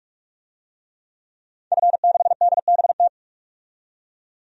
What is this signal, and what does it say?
Morse code, heard as a keyed tone, about 46 wpm, 700 Hz. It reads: F6DBT
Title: DXCC-100 Call Signs - 5 Characters - 45wpm